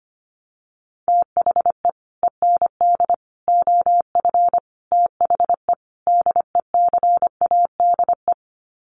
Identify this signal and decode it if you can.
Morse code, heard as a keyed tone, about 25 wpm, 700 Hz. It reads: THE END OF THE DECADE
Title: Sets of 5 Words Encore 25wpm